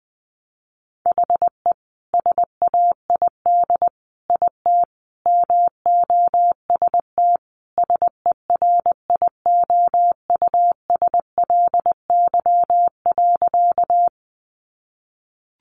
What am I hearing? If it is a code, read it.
Morse code, heard as a keyed tone, about 20 wpm, 700 Hz. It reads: HE SAID IT MOST SERIOUSLY.
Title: Sentences from Top 1200 Words 20wpm